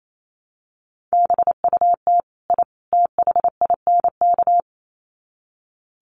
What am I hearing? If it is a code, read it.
Morse code, heard as a keyed tone, about 28 wpm, 700 Hz. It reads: BUT I THINK
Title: Sets of 3 Words 28wpm